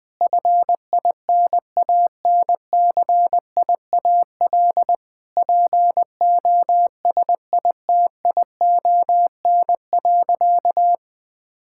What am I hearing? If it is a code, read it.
Morse code, heard as a keyed tone, about 20 wpm, 700 Hz. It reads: FINANCIAL POSITION.
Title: Sentences from Top 1000 Words - Review Part 5 - 20wpm